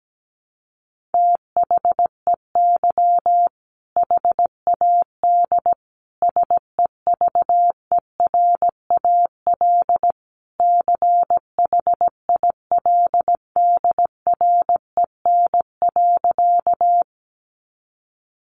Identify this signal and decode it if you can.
Morse code, heard as a keyed tone, about 17 wpm, 700 Hz. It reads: THEY HAD SEVERAL CHILDREN.
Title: Sentences from Top 400 Words 17wpm